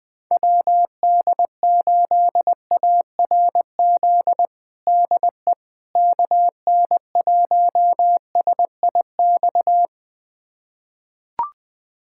Morse code, heard as a keyed tone, about 21 wpm, 700 Hz. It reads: WD8ARZ DE KN1SIX
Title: USA Callsign DE Callsign - 6 Characters - 20wpm